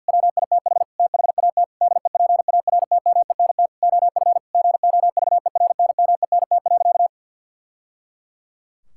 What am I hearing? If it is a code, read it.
Morse code, heard as a keyed tone, about 50 wpm, 700 Hz. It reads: WITH THAT DEPARTMENT OF GOVERNMENT.
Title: Sentences from Top 1000 Words - Review Part 5 - 50wpm